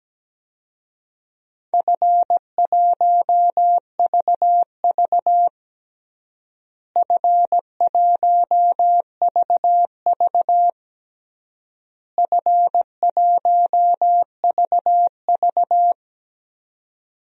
Morse code, heard as a keyed tone, about 17 wpm, 700 Hz. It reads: F1VV F1VV F1VV
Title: DXCC-100 Call Signs - 4 Characters - Repeated 3x - 17wpm